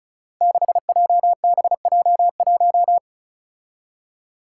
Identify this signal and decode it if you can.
Morse code, heard as a keyed tone, about 36 wpm, 700 Hz. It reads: 6JBJ1
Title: Five Challenging Letter-Number Combinations 35wpm